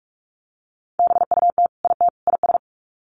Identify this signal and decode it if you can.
Morse code, heard as a keyed tone, about 44 wpm, 700 Hz. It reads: BUT IT IS
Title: Sets of 3 Words 45wpm